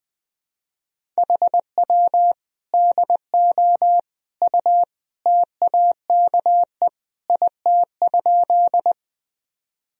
Morse code, heard as a keyed tone, about 21 wpm, 700 Hz. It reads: HW DO U TAKE IT?
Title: Sentences from Top 200 Words and Abbreviations 20wpm